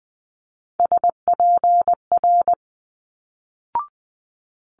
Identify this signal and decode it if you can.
Morse code, heard as a keyed tone, about 20 wpm, 700 Hz. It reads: SPR
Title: Multi-letter Phonemes Quick Fire 20wpm